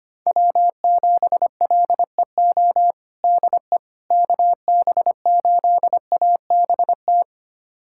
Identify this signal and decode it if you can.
Morse code, heard as a keyed tone, about 25 wpm, 700 Hz. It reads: W7LEO DE KB8ABT
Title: USA Callsign DE Callsign 25wpm